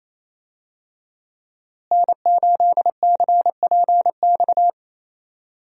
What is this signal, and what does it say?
Morse code, heard as a keyed tone, about 28 wpm, 700 Hz. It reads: N8CPX
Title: USA Call Signs - 5 Characters - Repeated 3x - 28wpm